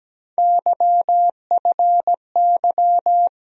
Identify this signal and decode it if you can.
Morse code, heard as a keyed tone, about 17 wpm, 700 Hz. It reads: YFY
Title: USA Call Sign Postfixes - 3 Letters 17wpm